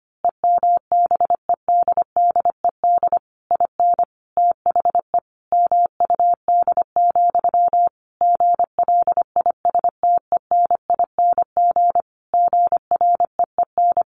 Morse code, heard as a keyed tone, about 25 wpm, 700 Hz. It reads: EMBEDDED IN THE MUD, GLISTENING GREEN
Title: A Sound Of Thunder 25wpm